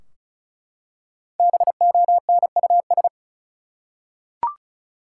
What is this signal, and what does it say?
Morse code, heard as a keyed tone, about 34 wpm, 700 Hz. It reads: BONUS
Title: Five Letter Words - 35wpm